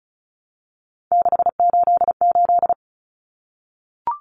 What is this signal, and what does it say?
Morse code, heard as a keyed tone, about 34 wpm, 700 Hz. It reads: 688
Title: Numbers - 3-Digit 35wpm